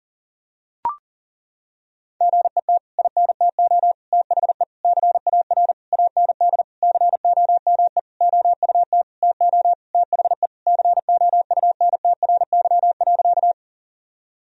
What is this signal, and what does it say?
Morse code, heard as a keyed tone, about 40 wpm, 700 Hz. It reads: GET INTO THE CAR AND COME OUT TO THE COUNTRY.
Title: Sentences from Top 1000 Words - Review Part 1 - 40wpm